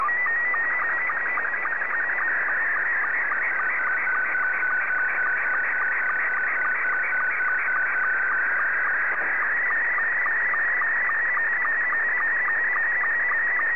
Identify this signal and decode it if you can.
SSTV (slow-scan television), Scottie 2.
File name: sstv / S2-VK3